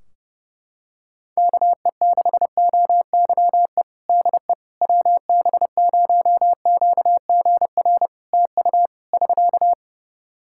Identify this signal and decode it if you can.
Morse code, heard as a keyed tone, about 30 wpm, 700 Hz. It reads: KE6OYE DE WB0QGR TU <SK>
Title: Practice QSO - Exchange 4 - Ending - 30wpm